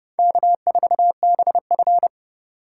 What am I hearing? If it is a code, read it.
Morse code, heard as a keyed tone, about 26 wpm, 700 Hz. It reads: K4BF